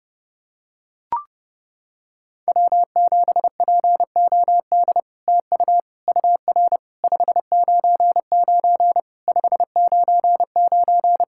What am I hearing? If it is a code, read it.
Morse code, heard as a keyed tone, about 30 wpm, 700 Hz. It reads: W7POD TU UR 599 599
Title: SOTA QSO Practice 30wpm